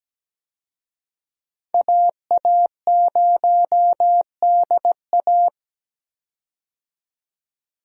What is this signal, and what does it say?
Morse code, heard as a keyed tone, about 17 wpm, 700 Hz. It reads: AA0DA
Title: USA Call Signs - 5 Characters - 17wpm